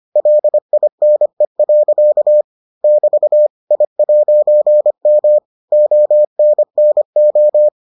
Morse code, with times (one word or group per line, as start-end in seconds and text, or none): 0.2-2.4 LINE.
2.8-3.5 <BT>
3.7-5.4 I'M
5.7-7.7 ONNO